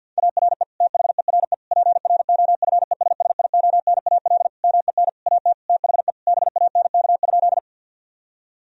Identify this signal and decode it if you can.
Morse code, heard as a keyed tone, about 50 wpm, 700 Hz. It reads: ARE THERE PROFESSIONAL MEN AT THE BANK?
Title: Sentences from Top 1000 Words - Review Part 4 - 50wpm